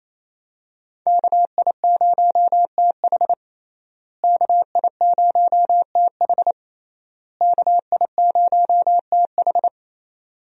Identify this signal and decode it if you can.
Morse code, heard as a keyed tone, about 28 wpm, 700 Hz. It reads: KI0TH KI0TH KI0TH
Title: USA Call Signs - 5 Characters - Repeated 3x - 28wpm